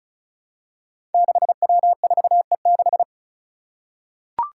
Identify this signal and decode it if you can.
Morse code, heard as a keyed tone, about 34 wpm, 700 Hz. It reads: 6W4E6
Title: Five Challenging Letter-Number Combinations 35wpm